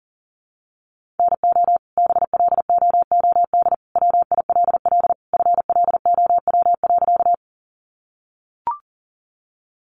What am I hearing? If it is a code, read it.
Morse code, heard as a keyed tone, about 40 wpm, 700 Hz. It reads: NO BLOOD WILL FLOW.
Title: Sentences from Top 1200 Words 40wpm